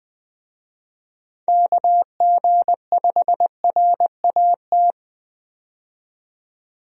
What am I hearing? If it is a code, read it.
Morse code, heard as a keyed tone, about 21 wpm, 700 Hz. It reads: KG5RAT
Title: USA Call Signs - 6 Characters - 20wpm